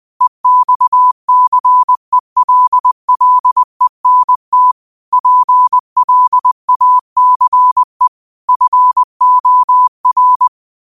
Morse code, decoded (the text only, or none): EXCELLENT PLACE FOR